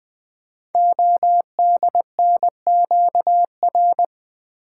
Morse code, heard as a keyed tone, about 20 wpm, 700 Hz. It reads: ODNQR
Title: Code Groups - 5 Letters - 20wpm